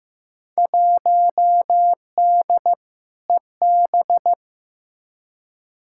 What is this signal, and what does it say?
Morse code, heard as a keyed tone, about 15 wpm, 700 Hz. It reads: 1D EB
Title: Field Day Exchanges - 15wpm